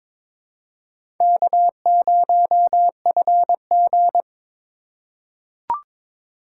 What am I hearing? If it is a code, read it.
Morse code, heard as a keyed tone, about 22 wpm, 700 Hz. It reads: K0FG
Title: USA Call Signs - 4 Characters - 4x Character Spacing - 22wpm